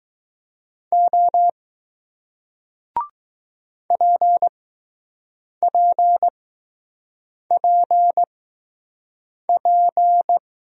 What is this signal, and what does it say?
Morse code, heard as a keyed tone, about 20 wpm, spaced as for 6 wpm, 700 Hz. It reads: O PPPP